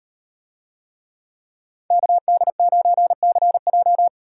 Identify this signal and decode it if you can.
Morse code, heard as a keyed tone, about 38 wpm, 700 Hz. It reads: KD9CJ